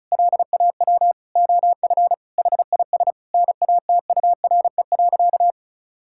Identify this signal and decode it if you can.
Morse code, heard as a keyed tone, about 34 wpm, 700 Hz. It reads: LAW OF HIS NATURE.
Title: Sentences from Top 1000 Words - Review Part 5 - 35wpm